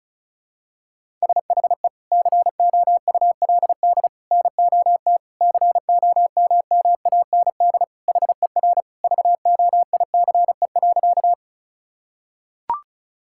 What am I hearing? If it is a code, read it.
Morse code, heard as a keyed tone, about 36 wpm, 700 Hz. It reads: SHE COULD NOT COMMAND HER VOICE.
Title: Sentences from Top 1600 Words 35wpm